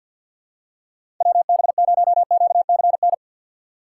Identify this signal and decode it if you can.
Morse code, heard as a keyed tone, about 47 wpm, 700 Hz. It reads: WB0QCN
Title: USA Call Signs - 6x Character Spacing - 50wpm